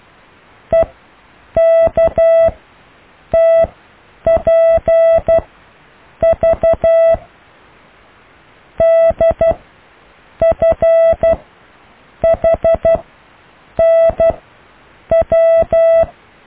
Morse code, heard as a keyed tone, about 12 wpm, spaced as for 7 wpm, 660 Hz. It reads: EKTPV DFHNW
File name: cw / IJLUM+CDRO+EKTPV+BFHNW